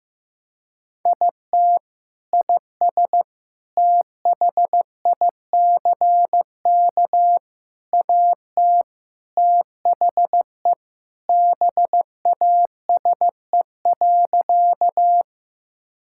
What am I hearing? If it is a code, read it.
Morse code, heard as a keyed tone, about 15 wpm, 700 Hz. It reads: IT IS THICK AT THE BASE.
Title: Sentences from Top 1600 Words 15wpm